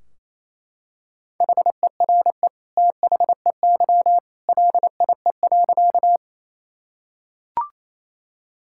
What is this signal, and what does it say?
Morse code, heard as a keyed tone, about 28 wpm, 700 Hz. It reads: HERE THEY LIE.
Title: Sentences from Top 1600 Words 28wpm